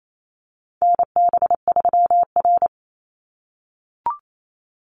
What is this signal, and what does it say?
Morse code, heard as a keyed tone, about 28 wpm, 700 Hz. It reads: NB3R
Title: Call Signs - 28wpm